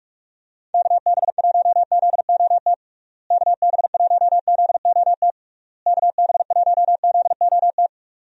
Morse code, heard as a keyed tone, about 42 wpm, 700 Hz. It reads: KB1ZOT KB1ZOT KB1ZOT
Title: USA Call Signs - 6 Characters - Repeated 3x - 45wpm